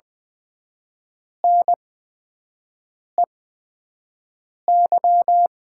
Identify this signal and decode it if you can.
Morse code, heard as a keyed tone, about 20 wpm, spaced as for 5 wpm, 700 Hz. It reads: NEY
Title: 8x Character Spacing - Top 500 Words - 20wpm